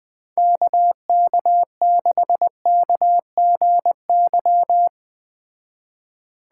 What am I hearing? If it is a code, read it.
Morse code, heard as a keyed tone, about 20 wpm, 700 Hz. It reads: KK6KGY
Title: Speed Racing - USA 2x3 Call Signs 17wpm